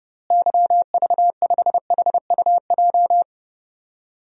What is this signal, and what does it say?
Morse code, heard as a keyed tone, about 32 wpm, 700 Hz. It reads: YV5HUJ
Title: DXCC-100 Call Signs - 6 Characters - Repeated 2x - 30wpm